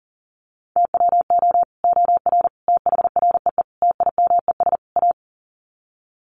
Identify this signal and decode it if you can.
Morse code, heard as a keyed tone, about 40 wpm, 700 Hz. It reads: TWO OR THREE TIMES A
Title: Sets of 5 Words Encore2 40wpm